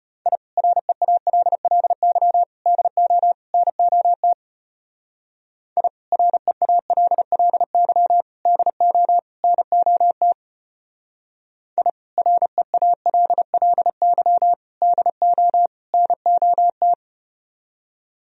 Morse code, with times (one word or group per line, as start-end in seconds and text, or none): 0.3-0.4 I
0.6-2.4 REALLY
2.7-3.3 DO
3.5-4.3 NOT
5.8-5.9 I
6.1-8.2 REALLY
8.5-9.2 DO
9.4-10.3 NOT
11.8-11.9 I
12.2-14.5 REALLY
14.8-15.7 DO
15.9-16.9 NOT